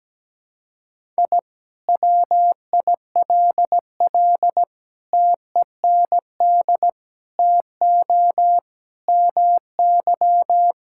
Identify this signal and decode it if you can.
Morse code, heard as a keyed tone, about 17 wpm, 700 Hz. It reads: I WILL TEND TO MY
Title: Sentences from Top 1000 Words - Review Part 5 - 17wpm